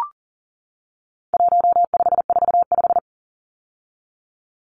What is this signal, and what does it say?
Morse code, heard as a keyed tone, about 40 wpm, 700 Hz. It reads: 1545
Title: Numbers - 4-Digit 40wpm